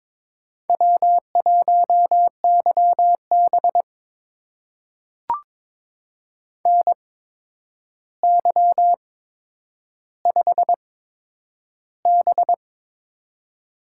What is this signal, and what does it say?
Morse code, heard as a keyed tone, about 22 wpm, spaced as for 9 wpm, 700 Hz. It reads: W1YB NY5B
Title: USA Call Signs - 4 Characters - 8x Character Spacing - 22wpm